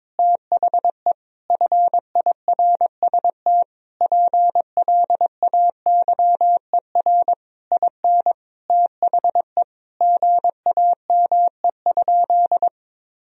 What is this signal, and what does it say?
Morse code, heard as a keyed tone, about 21 wpm, 700 Hz. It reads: THE FIRST PLAYER IN THE GAME?
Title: Sentences from Top 1000 Words - Review Part 4 - 22wpm